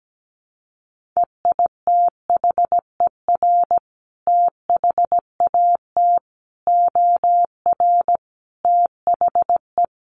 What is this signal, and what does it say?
Morse code, heard as a keyed tone, about 17 wpm, 700 Hz. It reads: EITHER THAT OR THE
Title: Sentences from Top 1400 Words 17wpm